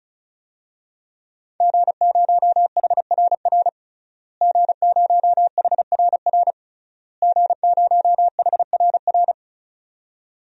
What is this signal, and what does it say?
Morse code, heard as a keyed tone, about 36 wpm, 700 Hz. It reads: G0HRR G0HRR G0HRR
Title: DXCC-100 Call Signs - 5 Characters - Repeated 3x - 35wpm